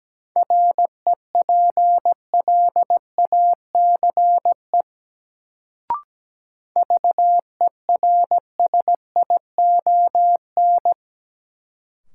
Morse code, heard as a keyed tone, about 17 wpm, 700 Hz. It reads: REPLACE VERSION
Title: Sentences from Top 1400 Words - New Word Drill Alt - 17wpm